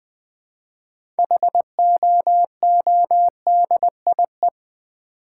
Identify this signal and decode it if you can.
Morse code, heard as a keyed tone, about 20 wpm, 700 Hz. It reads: HOODIE
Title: Difficult Words with Hints - 20wpm